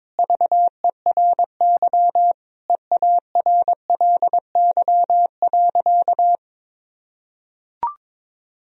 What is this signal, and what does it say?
Morse code, heard as a keyed tone, about 21 wpm, 700 Hz. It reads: VERY EARLY.
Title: Sentences from Top 1000 Words - Review Part 4 - 22wpm